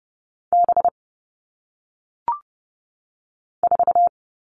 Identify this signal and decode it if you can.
Morse code, heard as a keyed tone, about 30 wpm, 700 Hz. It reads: B 4
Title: All Cut Numbers - 1-Digit 30wpm